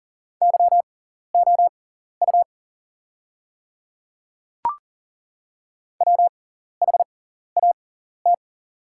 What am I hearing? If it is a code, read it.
Morse code, heard as a keyed tone, about 40 wpm, spaced as for 14 wpm, 690 Hz. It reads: YOU WHAT
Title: 6x Character Spacing - Top 100 Words - 40wpm